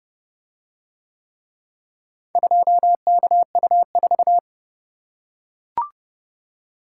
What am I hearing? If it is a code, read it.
Morse code, heard as a keyed tone, about 29 wpm, 700 Hz. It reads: 2KU4